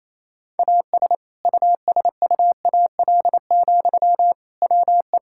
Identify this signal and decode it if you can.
Morse code, heard as a keyed tone, about 28 wpm, 700 Hz. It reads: AS USUAL, WE
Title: Sentences from Top 1600 Words 28wpm